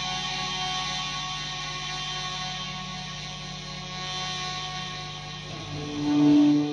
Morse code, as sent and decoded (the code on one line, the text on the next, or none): none